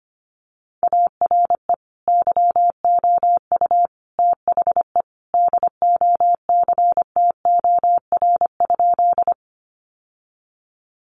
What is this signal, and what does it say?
Morse code, heard as a keyed tone, about 25 wpm, 700 Hz. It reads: ARE YOU THE DOCTOR?
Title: Sentences from Top 1500 Words 25wpm